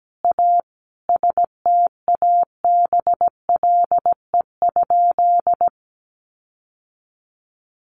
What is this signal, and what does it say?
Morse code, heard as a keyed tone, about 17 wpm, 700 Hz. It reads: A STABLE?
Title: Sentences from Top 1500 Words 17wpm